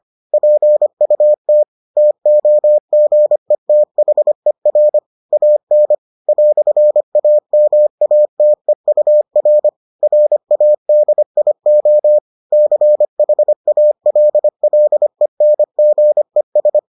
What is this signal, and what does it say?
Morse code, heard as a keyed tone, about 25 wpm, 600 Hz. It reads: PUT TOGETHER AN "AMATEUR RADIO CHALLENGES